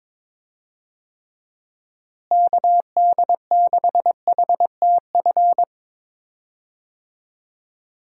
Morse code, heard as a keyed tone, about 22 wpm, 700 Hz. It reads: KD6HTF
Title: USA Call Signs - 6 Characters - Repeated 2x - 22wpm